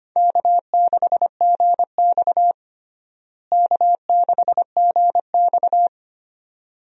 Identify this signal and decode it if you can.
Morse code, heard as a keyed tone, about 25 wpm, 700 Hz. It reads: K6GX K6GX
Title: USA Call Signs - 4 Characters - Repeated 2x - 25wpm